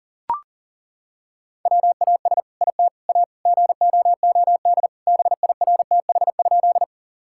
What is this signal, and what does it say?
Morse code, heard as a keyed tone, about 40 wpm, 700 Hz. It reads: WAS IT A GOOD BIRTH?
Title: Sentences from Top 1000 Words - Review Part 5 - 40wpm